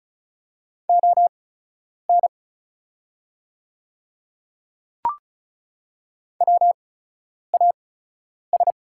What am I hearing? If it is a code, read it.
Morse code, heard as a keyed tone, about 36 wpm, spaced as for 10 wpm, 690 Hz. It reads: ON WAS
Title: 8x Character Spacing - Top 100 Words - 35wpm